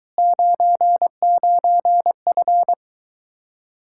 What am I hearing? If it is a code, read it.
Morse code, heard as a keyed tone, about 23 wpm, 700 Hz. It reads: 99F